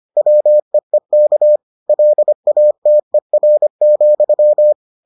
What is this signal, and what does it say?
Morse code, heard as a keyed tone, about 25 wpm, 600 Hz. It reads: WEEK LATER,